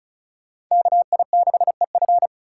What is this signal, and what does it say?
Morse code, heard as a keyed tone, about 45 wpm, 700 Hz. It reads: KI6EF